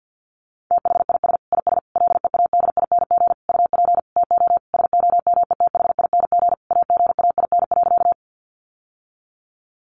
Morse code, heard as a keyed tone, about 50 wpm, 700 Hz. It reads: THIS IS LEADING UP TO SOMETHING AGAIN.
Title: Sentences from Top 1100 Words 50wpm